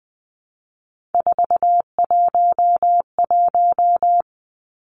Morse code, heard as a keyed tone, about 20 wpm, 700 Hz. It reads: 411
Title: Numbers - 3-Digit 20wpm